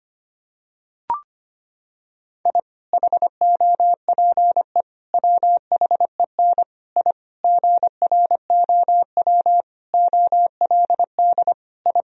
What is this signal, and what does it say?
Morse code, heard as a keyed tone, about 25 wpm, 700 Hz. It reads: I HOPE WHEN I GROW OLD I
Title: Sentences from Top 1000 Words - Review Part 1 - 25wpm